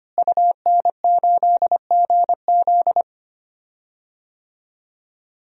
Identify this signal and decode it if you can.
Morse code, heard as a keyed tone, about 25 wpm, 700 Hz. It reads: UN8GZ
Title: DXCC-100 Call Signs - 5 Characters - Repeated 3x - 25wpm